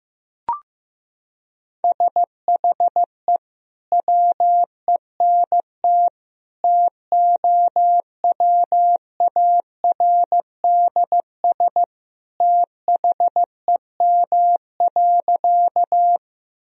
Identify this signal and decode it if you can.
Morse code, heard as a keyed tone, about 16 wpm, 700 Hz. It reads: SHE WENT TOWARDS THEM.
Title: Sentences from Top 1000 Words - Review Part 2 - 15wpm